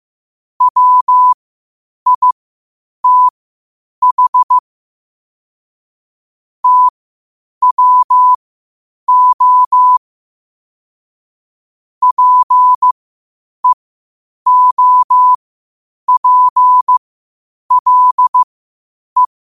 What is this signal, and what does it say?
Morse code, heard as a keyed tone, about 15 wpm, spaced as for 8 wpm, 1000 Hz. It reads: WITH TWO PEOPLE
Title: 8 WPM morse code quotes for Thu, 29 Feb 2024